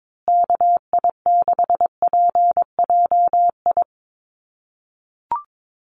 Morse code, heard as a keyed tone, about 22 wpm, 700 Hz. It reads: KI6PJI
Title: USA Call Signs Repeated 3x 22wpm